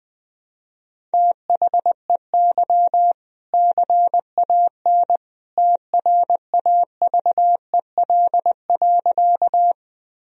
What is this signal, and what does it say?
Morse code, heard as a keyed tone, about 20 wpm, 700 Hz. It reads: THEY CAN TRAVEL.
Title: Sentences from Top 1000 Words - Review Part 2 - 20wpm